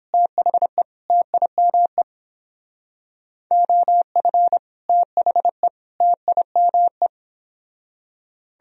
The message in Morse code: THE TIME OF THE TIME

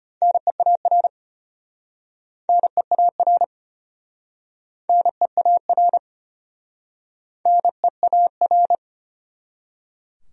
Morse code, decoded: NEAR NEAR NEAR NEAR

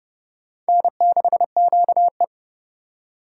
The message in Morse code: N6QE

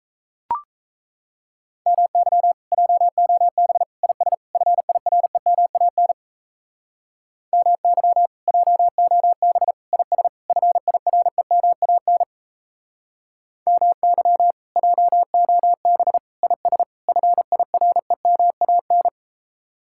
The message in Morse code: MY JOB IS FIREMAN MY JOB IS FIREMAN MY JOB IS FIREMAN

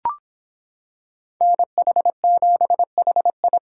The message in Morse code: NH7HI

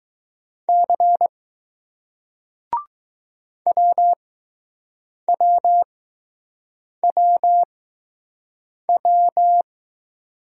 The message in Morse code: C WWWW

